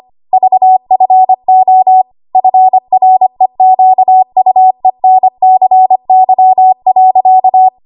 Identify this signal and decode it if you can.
Morse code, heard as a keyed tone, about 25 wpm, 750 Hz. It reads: VFO FREQUENCY.
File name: Archive / 25 WPM / 25WPM